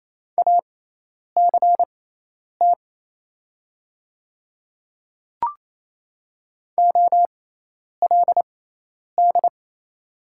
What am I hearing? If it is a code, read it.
Morse code, heard as a keyed tone, about 28 wpm, spaced as for 10 wpm, 700 Hz. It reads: ACT OLD
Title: 6x Character Spacing - Top 200 Words - 28wpm